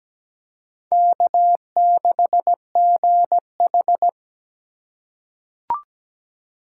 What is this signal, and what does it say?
Morse code, heard as a keyed tone, about 17 wpm, 700 Hz. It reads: K6GH
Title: USA Call Signs - 4 Characters - 4x Character Spacing - 17wpm